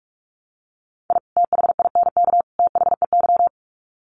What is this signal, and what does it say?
Morse code, heard as a keyed tone, about 44 wpm, 700 Hz. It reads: I THINK THEY
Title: Sets of 3 Words 45wpm